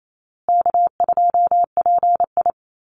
Morse code, heard as a keyed tone, about 28 wpm, 700 Hz. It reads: K2PI
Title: Easy Call Signs Repeated 3x - 28wpm